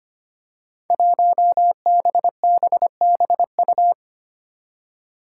Morse code, heard as a keyed tone, about 25 wpm, 700 Hz. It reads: 1BBBU
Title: Five Challenging Letter-Number Combinations 25wpm